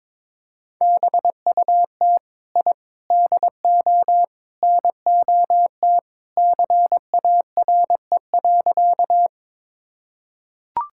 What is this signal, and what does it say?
Morse code, heard as a keyed tone, about 21 wpm, 700 Hz. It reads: BUT I DO NOT CARE.
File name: practice / 22wpm / Sentences with 4-Word Phrases - 22wpm